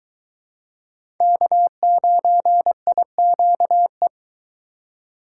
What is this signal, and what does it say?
Morse code, heard as a keyed tone, about 22 wpm, 700 Hz. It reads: K9IQE